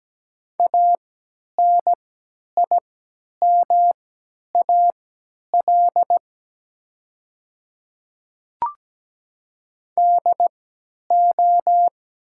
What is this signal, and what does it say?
Morse code, heard as a keyed tone, about 17 wpm, spaced as for 10 wpm, 700 Hz. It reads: ANIMAL DO
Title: 3x Character Spacing - Top 200 Words - 17wpm